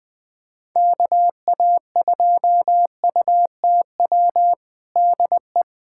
Morse code, heard as a keyed tone, about 21 wpm, 700 Hz. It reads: KA2UTW DE
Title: USA Callsign DE Callsign 20wpm